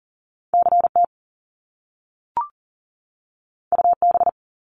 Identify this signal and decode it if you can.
Morse code, heard as a keyed tone, about 42 wpm, 700 Hz. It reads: CT UB
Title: Top Two Letter Combinations in Words 40wpm